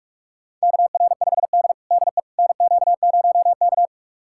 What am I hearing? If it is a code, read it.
Morse code, heard as a keyed tone, about 46 wpm, 700 Hz. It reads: KR5D DE NQ0K
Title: USA Callsign DE Callsign - 4 Characters - 45wpm